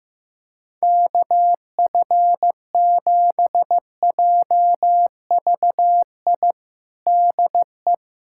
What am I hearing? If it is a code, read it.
Morse code, heard as a keyed tone, about 15 wpm, 700 Hz. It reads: KF7JVI DE
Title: USA Callsign DE Callsign - 6 Characters - 15wpm